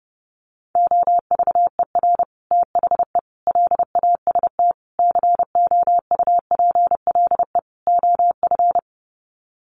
Morse code, heard as a keyed tone, about 30 wpm, 700 Hz. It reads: OVER THE LAST COUPLE OF
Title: Sets of 5 Words Encore 30wpm